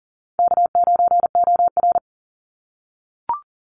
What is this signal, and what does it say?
Morse code, heard as a keyed tone, about 41 wpm, 700 Hz. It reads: K9OR
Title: Call Signs - 40wpm